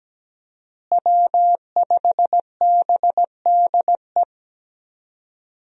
Morse code, heard as a keyed tone, about 17 wpm, 700 Hz. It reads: W5BDE
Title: USA Call Signs - 5 Characters - Repeated 2x - 17wpm